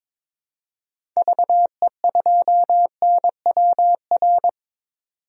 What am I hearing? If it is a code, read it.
Morse code, heard as a keyed tone, about 22 wpm, 700 Hz. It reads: VE2NWR
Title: DXCC-100 Call Signs - 6 Characters - 22wpm